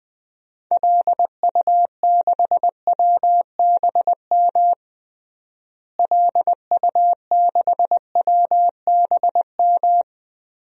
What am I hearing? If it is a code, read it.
Morse code, heard as a keyed tone, about 20 wpm, 700 Hz. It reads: LU6WBM LU6WBM
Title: DXCC-100 Call Signs - 6 Characters - Repeated 2x - 20wpm